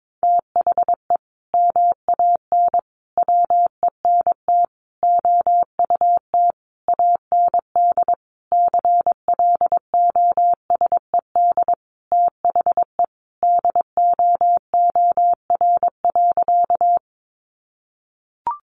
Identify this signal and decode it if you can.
Morse code, heard as a keyed tone, about 22 wpm, 700 Hz. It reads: THE MAN WENT OUT AND CLOSED THE DOOR.
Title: Sentences from Top 1400 Words 22wpm